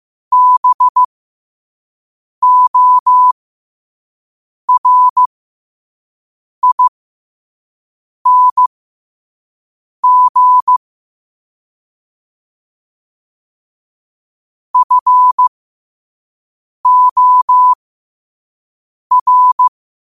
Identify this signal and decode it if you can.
Morse code, heard as a keyed tone, about 15 wpm, spaced as for 5 wpm, 1000 Hz. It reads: BORING FOR